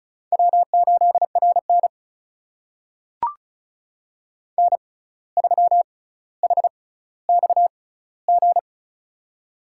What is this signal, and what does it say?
Morse code, heard as a keyed tone, about 36 wpm, spaced as for 17 wpm, 700 Hz. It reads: W8RN N3HXG
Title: USA Call Signs - 6x Character Spacing - 35wpm